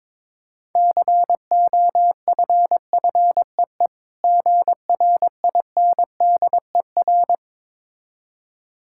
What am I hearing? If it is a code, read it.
Morse code, heard as a keyed tone, about 23 wpm, 700 Hz. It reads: COFFEE GRINDER
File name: Easy Words - Speed-Racing with Hints - 22wpm